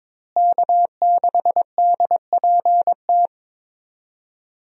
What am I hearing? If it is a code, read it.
Morse code, heard as a keyed tone, about 22 wpm, 700 Hz. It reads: K6DPT
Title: USA Call Signs - 5 Characters - Repeated 2x - 22wpm